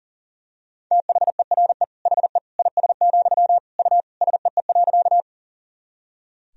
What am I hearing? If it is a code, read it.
Morse code, heard as a keyed tone, about 40 wpm, 700 Hz. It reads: THERE HE IS, U SEE.
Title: Sentences from Top 100 Words and Abbreviations 40wpm